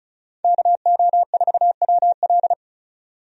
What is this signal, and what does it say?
Morse code, heard as a keyed tone, about 36 wpm, 700 Hz. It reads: KO4WL
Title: USA Call Signs - 5 Characters - 35wpm